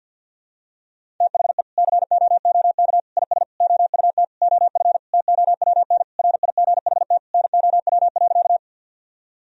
Morse code, heard as a keyed tone, about 48 wpm, 700 Hz. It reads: THE COOK IS OUT OF TOWN RIGHT NOW.
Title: Sentences from Top 1000 Words - Review Part 4 - 50wpm